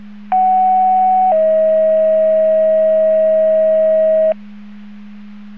Two-tone page, heard as a one-second tone, then a three-second tone, about 750 then 640 Hz. It is heard from 0.3 to 4.3 s.